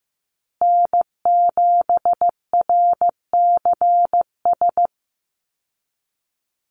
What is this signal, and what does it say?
Morse code, heard as a keyed tone, about 15 wpm, 700 Hz. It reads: N7RCS
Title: Call Signs Repeated 3x - 15wpm